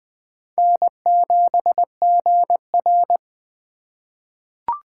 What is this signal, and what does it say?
Morse code, heard as a keyed tone, about 20 wpm, 700 Hz. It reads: N7GR
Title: USA Call Signs - 4 Characters - 2x Character Spacing - 20wpm